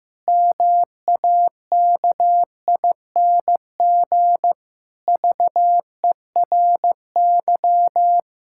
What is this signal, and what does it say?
Morse code, heard as a keyed tone, about 15 wpm, 700 Hz. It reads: MAKING VERY
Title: Pangram Challenge v2 - 15wpm